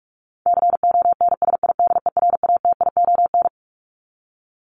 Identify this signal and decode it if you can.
Morse code, heard as a keyed tone, about 44 wpm, 700 Hz. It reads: CONSIDERATION
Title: Top 1251-1500 Words 45wpm